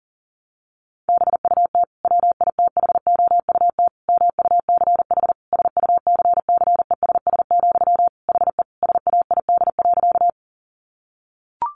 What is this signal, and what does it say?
Morse code, heard as a keyed tone, about 40 wpm, 700 Hz. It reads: BUT WITHOUT MUCH SUCCESS, HE SAID.
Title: Sentences from Top 900 Words 40wpm